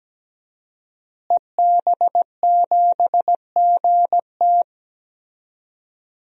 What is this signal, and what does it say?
Morse code, heard as a keyed tone, about 17 wpm, 700 Hz. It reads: EB7GT
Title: DXCC-100 Call Signs - 5 Characters - Repeated 2x - 17wpm